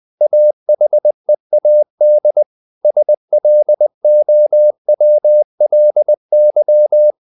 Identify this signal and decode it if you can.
Morse code, heard as a keyed tone, about 20 wpm, 600 Hz. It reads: AHEAD SLOWLY
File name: the old man and the sea part 2 20wpm